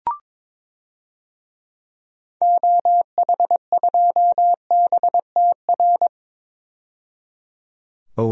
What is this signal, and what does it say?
Morse code, heard as a keyed tone, about 22 wpm, 700 Hz. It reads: OH2BTR
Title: DXCC-100 Call Signs - 6 Characters - 22wpm